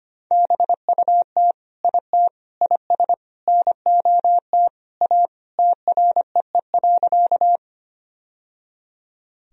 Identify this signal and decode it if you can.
Morse code, heard as a keyed tone, about 25 wpm, 700 Hz. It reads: BUT IT IS NOT A TREE.
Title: Sentences from Top 1000 Words - Review Part 1 - 25wpm